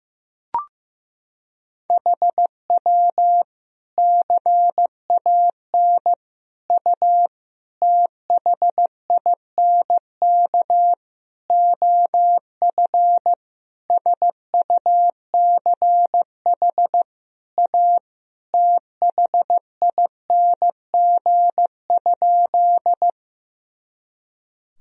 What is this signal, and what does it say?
Morse code, heard as a keyed tone, about 15 wpm, 700 Hz. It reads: HW CAN U THINK OF SUCH A THING?
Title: Sentences from Top 200 Words and Abbreviations 15wpm